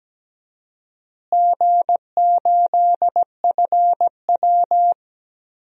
Morse code, heard as a keyed tone, about 17 wpm, 700 Hz. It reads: G8FW
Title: DXCC-100 Call Signs - 4 Characters - Repeated 3x - 17wpm